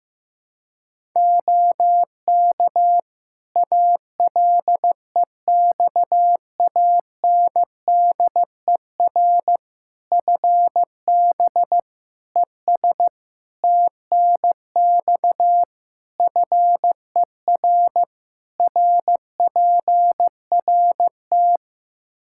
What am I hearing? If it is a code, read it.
Morse code, heard as a keyed tone, about 15 wpm, 700 Hz. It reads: OK ALEXANDER FB ES TNX FER RPRT
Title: Practice QSO - Exchange 2 - Info - 15wpm